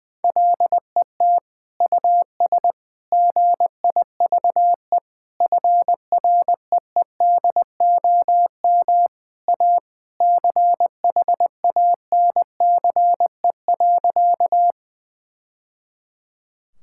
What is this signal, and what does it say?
Morse code, heard as a keyed tone, about 21 wpm, 700 Hz. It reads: LET US GIVE FREEDOM A CHANCE.
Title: Sentences from Top 1000 Words - Review Part 5 - 20wpm